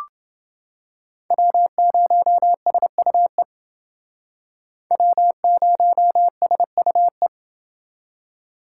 Morse code, heard as a keyed tone, about 28 wpm, 700 Hz. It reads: W0SUE W0SUE